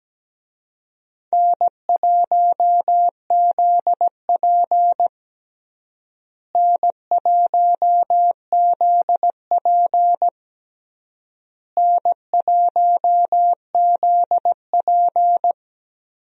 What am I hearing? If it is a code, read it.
Morse code, heard as a keyed tone, about 17 wpm, 700 Hz. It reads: N1ZP N1ZP N1ZP
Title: USA Call Signs - 4 Characters - Repeated 3x - 17wpm